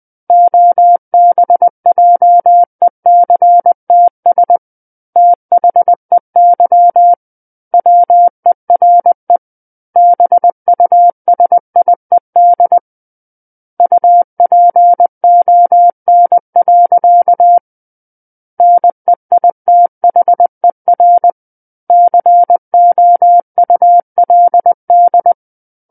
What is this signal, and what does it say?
Morse code, heard as a keyed tone, about 20 wpm, 700 Hz. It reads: OBJECTS THEY WERE BUSIED UPON. NEITHER COULD
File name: 20 WPM